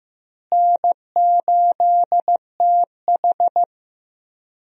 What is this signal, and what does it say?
Morse code, heard as a keyed tone, about 15 wpm, 700 Hz. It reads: N8TH
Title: USA Call Signs - 4 Characters - Repeated 3x - 15wpm